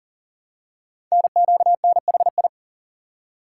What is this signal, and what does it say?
Morse code, heard as a keyed tone, about 40 wpm, 700 Hz. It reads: NQNHI
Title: Code Groups - 5 Letters - 40wpm